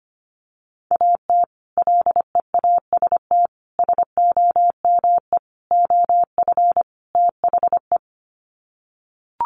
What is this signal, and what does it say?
Morse code, heard as a keyed tone, about 25 wpm, 700 Hz. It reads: AT LEAST SOME OF THE
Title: Sets of 5 Words Encore 25wpm